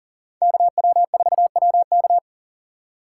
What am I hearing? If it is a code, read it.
Morse code, heard as a keyed tone, about 40 wpm, 700 Hz. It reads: KW4WK
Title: USA Call Signs - 5 Characters - 3x Character Spacing - 40wpm